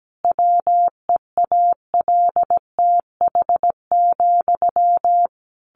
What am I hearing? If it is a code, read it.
Morse code, heard as a keyed tone, about 17 wpm, 700 Hz. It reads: WEALTH,
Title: Sentences from Top 1200 Words 17wpm